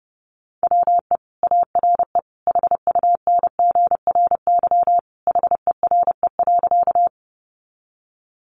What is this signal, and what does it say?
Morse code, heard as a keyed tone, about 30 wpm, 700 Hz. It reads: WE ARE HUNGRY HERE.
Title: Sentences from Top 1400 Words 30wpm